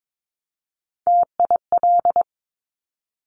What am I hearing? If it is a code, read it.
Morse code, heard as a keyed tone, about 22 wpm, 700 Hz. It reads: TIL
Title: Top Three Letter Combinations in Words 22wpm